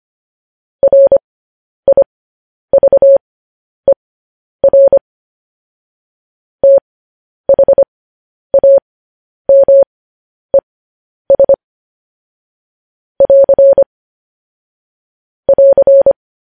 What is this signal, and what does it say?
Morse code, heard as a keyed tone, about 25 wpm, spaced as for 10 wpm, 550 Hz. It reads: RIVER THAMES <AR> <AR>
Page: News Headlines in Morse Code at 10 WPM